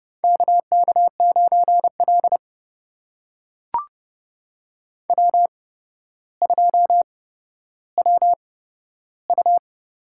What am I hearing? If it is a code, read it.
Morse code, heard as a keyed tone, about 30 wpm, spaced as for 13 wpm, 700 Hz. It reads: KK9L W2WU
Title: USA Call Signs - 8x Character Spacing - 30wpm